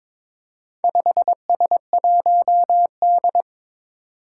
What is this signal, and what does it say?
Morse code, heard as a keyed tone, about 22 wpm, 700 Hz. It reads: 5S1D
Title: Four Challenging Letter-Number Combinations 22wpm